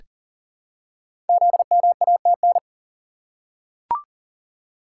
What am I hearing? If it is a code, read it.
Morse code, heard as a keyed tone, about 41 wpm, 700 Hz. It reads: ZMATN
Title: Code Groups - 5 Letters - 40wpm